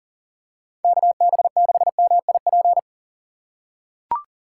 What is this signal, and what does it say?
Morse code, heard as a keyed tone, about 40 wpm, 700 Hz. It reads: KB6MIP
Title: USA Call Signs - 6 Characters - 4x Character Spacing - 40wpm